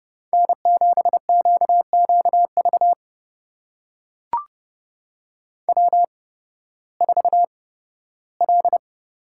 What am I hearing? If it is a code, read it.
Morse code, heard as a keyed tone, about 29 wpm, spaced as for 16 wpm, 700 Hz. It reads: N7QQV W4L
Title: USA Call Signs - 5 Characters - 8x Character Spacing - 30wpm